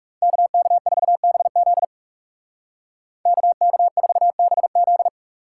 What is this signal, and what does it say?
Morse code, heard as a keyed tone, about 36 wpm, 700 Hz. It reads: KK4BZ KK4BZ
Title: Speed Racing - USA 2x2 Call Signs 30wpm